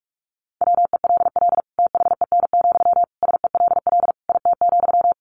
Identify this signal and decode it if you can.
Morse code, heard as a keyed tone, about 44 wpm, 700 Hz. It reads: WELL THEN, SELL IT,
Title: Sentences from Top 1100 Words 45wpm